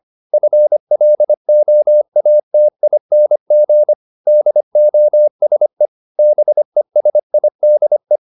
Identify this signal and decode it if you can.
Morse code, heard as a keyed tone, about 25 wpm, 600 Hz. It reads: FLOATING DOSE BESIDE
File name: the old man and the sea part 1 25wpm